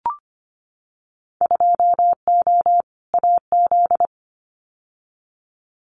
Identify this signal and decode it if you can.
Morse code, heard as a keyed tone, about 25 wpm, 700 Hz. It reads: 2O AZ
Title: Winter Field Day Exchanges 25wpm